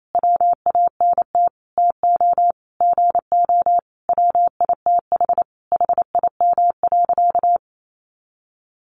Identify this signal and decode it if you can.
Morse code, heard as a keyed tone, about 28 wpm, 700 Hz. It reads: WANT TO GO WITH HIM.
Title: Sentences from Top 1300 Words 28wpm